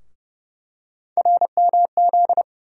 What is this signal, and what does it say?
Morse code, heard as a keyed tone, about 30 wpm, 700 Hz. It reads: RMZ
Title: USA Call Sign Postfixes - 3 Letters 30wpm